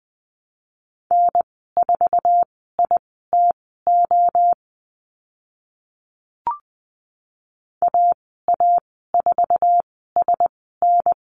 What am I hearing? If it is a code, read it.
Morse code, heard as a keyed tone, about 20 wpm, spaced as for 14 wpm, 700 Hz. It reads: N4ITO AA4SN
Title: 048 Lesson 9g - Number 4 - Call Signs 20wpm